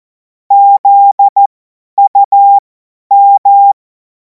Morse code, heard as a keyed tone, about 14 wpm, spaced as for 10 wpm, 800 Hz. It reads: ZUM
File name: Morsekurs / Witze / Witz 14. April 2026 50 F70